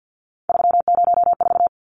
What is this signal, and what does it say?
Morse code, heard as a keyed tone, about 49 wpm, 700 Hz. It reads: 304
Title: Numbers - 3-Digit 50wpm